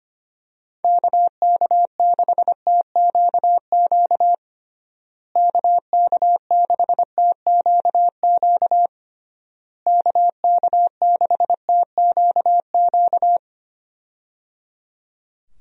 Morse code, heard as a keyed tone, about 25 wpm, 700 Hz. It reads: KK6TQQ KK6TQQ KK6TQQ
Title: USA Call Signs - 6 Characters - Repeated 3x - 25wpm